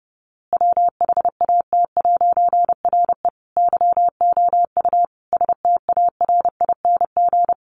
Morse code, heard as a keyed tone, about 30 wpm, 700 Hz. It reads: WHAT'RE YOU STARING
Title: A Sound Of Thunder 30wpm